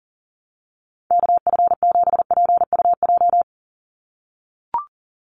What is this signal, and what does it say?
Morse code, heard as a keyed tone, about 40 wpm, 700 Hz. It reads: KF7PUJ
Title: USA Call Signs 40wpm